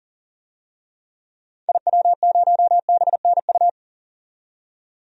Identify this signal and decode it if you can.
Morse code, heard as a keyed tone, about 40 wpm, 700 Hz. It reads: IW0BNU
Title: DXCC-100 Call Signs - 6 Characters - 40wpm